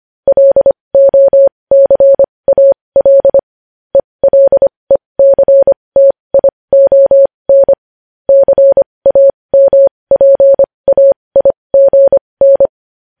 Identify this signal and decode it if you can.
Morse code, heard as a keyed tone, about 25 wpm, spaced as for 20 wpm, 550 Hz. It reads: LOCAL ELECTION CAMPAIGN
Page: news-headlines-in-morse-code-at-20-wpm